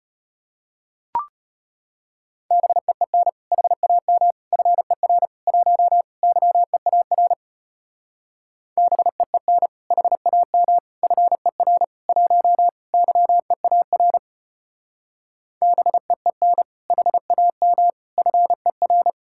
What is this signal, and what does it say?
Morse code, heard as a keyed tone, about 30 wpm, 700 Hz. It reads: BEEN HAM FER 1 YEAR BEEN HAM FER 1 YEAR BEEN HAM FER
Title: Speed-Racing Common QSO Phrases - 25wpm